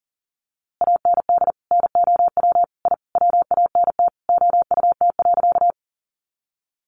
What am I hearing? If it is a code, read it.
Morse code, heard as a keyed tone, about 37 wpm, 700 Hz. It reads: AND NOW I WANT OUT.
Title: A Sound Of Thunder 40wpm